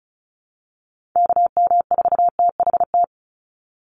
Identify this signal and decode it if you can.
Morse code, heard as a keyed tone, about 34 wpm, 700 Hz. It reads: KM4THT
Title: USA Call Signs 35wpm